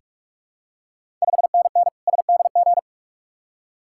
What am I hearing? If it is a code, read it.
Morse code, heard as a keyed tone, about 46 wpm, 690 Hz. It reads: 5NN SDG
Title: Ontario QSO Party Exchange 45wpm